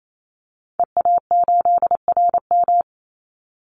Morse code, heard as a keyed tone, about 28 wpm, 700 Hz. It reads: EA8RM
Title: Call Signs Repeated 2x - 28wpm